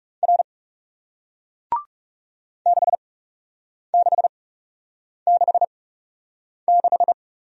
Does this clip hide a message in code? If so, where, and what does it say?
Yes: Morse code from 0.2 to 7.1 s, heard as a keyed tone, about 40 wpm, spaced as for 8 wpm, 700 Hz. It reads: R 6666